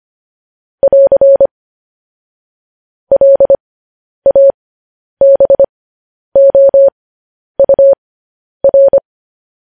Morse code, heard as a keyed tone, about 25 wpm, spaced as for 10 wpm, 550 Hz. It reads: <AR> LABOUR
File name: podcast / News headlines-1x-25x10